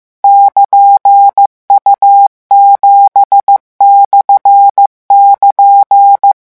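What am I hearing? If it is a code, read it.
Morse code, heard as a keyed tone, about 15 wpm, 800 Hz. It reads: <KN>U7/<KN>
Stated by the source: Morse revision lesson at 14 words per minute.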